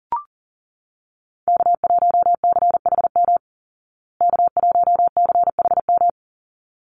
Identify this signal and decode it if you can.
Morse code, heard as a keyed tone, about 40 wpm, 700 Hz. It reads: K1CHM K1CHM
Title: Call Signs Repeated 3x - 40wpm